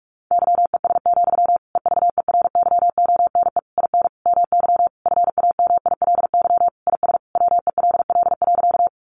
Morse code, heard as a keyed tone, about 44 wpm, 700 Hz. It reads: YES, EVERYONE IN MY FAMILY IS WELL.
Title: Sentences from Top 800 Words 45wpm